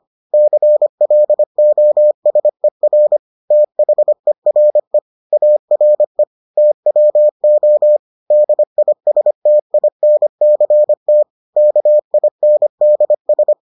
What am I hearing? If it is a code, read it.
Morse code, heard as a keyed tone, about 25 wpm, 600 Hz. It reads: CLOSER THERE ARE TWO DISTINCT KINDS